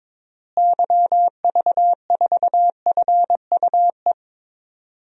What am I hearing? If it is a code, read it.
Morse code, heard as a keyed tone, about 22 wpm, 700 Hz. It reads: YV4FUE
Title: DXCC-100 Call Signs - 6 Characters - Repeated 2x - 22wpm